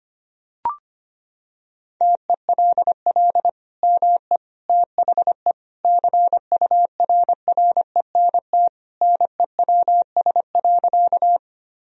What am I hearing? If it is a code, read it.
Morse code, heard as a keyed tone, about 25 wpm, 700 Hz. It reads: TELL ME THE CURRENT NEWS.
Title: Sentences from Top 1000 Words - Review Part 4 - 25wpm